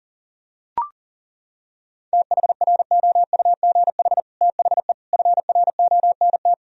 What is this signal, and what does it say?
Morse code, heard as a keyed tone, about 40 wpm, 700 Hz. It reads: THROUGH THE FRONT
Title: Sentences from Top 1000 Words - Review Part 2 - 40wpm